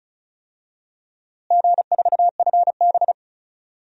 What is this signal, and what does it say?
Morse code, heard as a keyed tone, about 36 wpm, 700 Hz. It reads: G4FB
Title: DXCC-100 Call Signs - 4 Characters - Repeated 2x - 35wpm